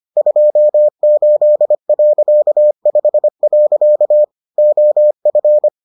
Morse code, heard as a keyed tone, about 25 wpm, 600 Hz. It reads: 28.5. OF